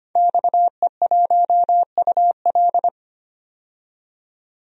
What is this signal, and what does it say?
Morse code, heard as a keyed tone, about 25 wpm, 700 Hz. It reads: XE1UL
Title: DXCC-100 Call Signs - 5 Characters - 25wpm